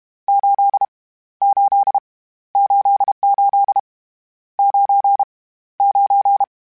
Morse code, heard as a keyed tone, about 31 wpm, 800 Hz. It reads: 8 8 88 9 9
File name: Misc / Misc-mp3s / Numb @30